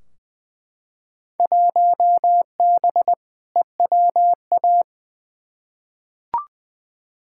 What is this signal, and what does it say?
Morse code, heard as a keyed tone, about 20 wpm, 700 Hz. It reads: 1B EWA
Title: Field Day Exchanges Encore - 20wpm